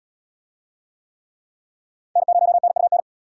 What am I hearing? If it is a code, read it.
Morse code, heard as a keyed tone, about 73 wpm, 700 Hz. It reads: N0NUN